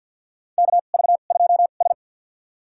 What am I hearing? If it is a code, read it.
Morse code, heard as a keyed tone, about 52 wpm, spaced as for 37 wpm, 700 Hz. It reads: KV2S
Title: USA Call Signs - 2x Character Spacing - 50wpm